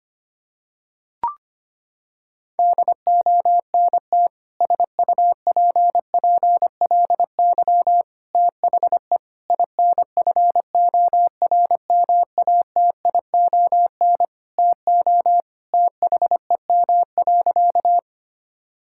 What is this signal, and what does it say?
Morse code, heard as a keyed tone, about 25 wpm, 700 Hz. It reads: DONT SUPPLY THE INFORMATION TO THEM.
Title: Sentences from Top 1000 Words - Review Part 5 - 25wpm